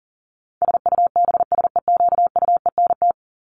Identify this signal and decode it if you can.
Morse code, heard as a keyed tone, about 40 wpm, 700 Hz. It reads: SUBSEQUENT
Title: LY Suffix - Top 100 Words with Base Words 40wpm